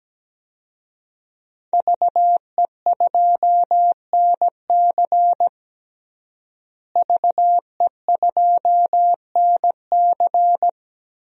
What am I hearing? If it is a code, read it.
Morse code, heard as a keyed tone, about 17 wpm, 700 Hz. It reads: VE2NC VE2NC
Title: DXCC-100 Call Signs - 5 Characters - Repeated 2x - 17wpm